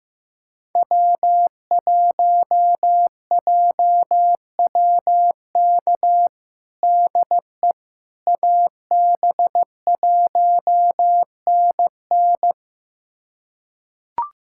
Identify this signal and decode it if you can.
Morse code, heard as a keyed tone, about 15 wpm, 700 Hz. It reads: W1JWK DE AB1NN
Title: USA Callsign DE Callsign - 5 Characters - 15wpm